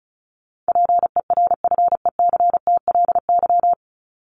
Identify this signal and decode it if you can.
Morse code, heard as a keyed tone, about 34 wpm, 700 Hz. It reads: PERFECTLY
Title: LY Suffix - Top 100 Words with Base Words 35wpm